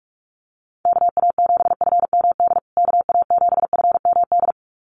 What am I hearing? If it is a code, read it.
Morse code, heard as a keyed tone, about 44 wpm, 700 Hz. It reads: KA7FMD KA7FMD
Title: USA Call Signs Repeated 2x 45wpm